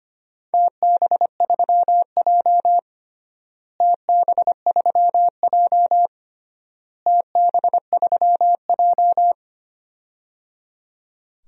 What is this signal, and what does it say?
Morse code, heard as a keyed tone, about 25 wpm, 700 Hz. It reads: TB3J TB3J TB3J
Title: DXCC-100 Call Signs - 4 Characters - Repeated 3x - 25wpm